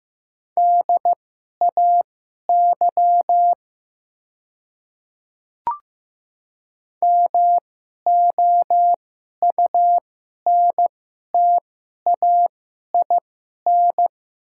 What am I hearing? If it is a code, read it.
Morse code, heard as a keyed tone, about 15 wpm, spaced as for 11 wpm, 690 Hz. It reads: DAY MOUNTAIN
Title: 2x Character Spacing - Top 300 Words - 15wpm